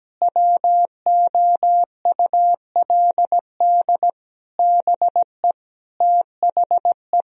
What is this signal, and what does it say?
Morse code, heard as a keyed tone, about 17 wpm, 700 Hz. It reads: WOULD BE THE
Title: Sentences from Top 1000 Words - Review Part 2 - 17wpm